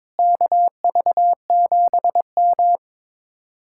Morse code, heard as a keyed tone, about 22 wpm, 700 Hz. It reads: KV7M
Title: Speed-Racing Common QSO Phrases - 22wpm